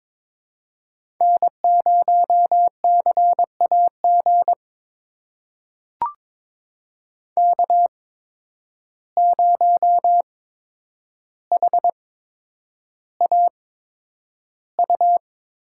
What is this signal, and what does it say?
Morse code, heard as a keyed tone, about 22 wpm, spaced as for 9 wpm, 700 Hz. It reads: N0CAG K0HAU
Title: USA Call Signs - 5 Characters - 8x Character Spacing - 22wpm